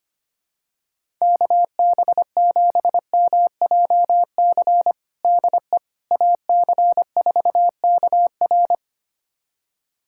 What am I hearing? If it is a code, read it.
Morse code, heard as a keyed tone, about 25 wpm, 700 Hz. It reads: KB7MJC DE AC4KR
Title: USA Callsign DE Callsign 25wpm